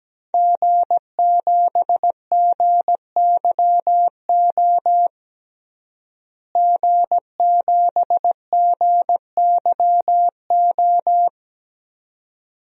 Morse code, heard as a keyed tone, about 17 wpm, 700 Hz. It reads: G7GYO G7GYO
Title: DXCC-100 Call Signs - 5 Characters - Repeated 2x - 17wpm